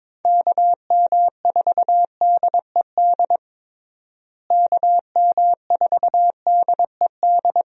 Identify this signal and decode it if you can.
Morse code, heard as a keyed tone, about 22 wpm, 700 Hz. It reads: KM4DED KM4DED
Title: USA Call Signs - 6 Characters - Repeated 3x - 22wpm